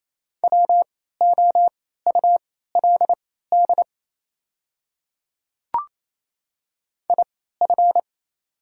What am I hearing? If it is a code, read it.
Morse code, heard as a keyed tone, about 28 wpm, spaced as for 16 wpm, 690 Hz. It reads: WOULD IF
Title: 3x Character Spacing - Top 100 Words - 28wpm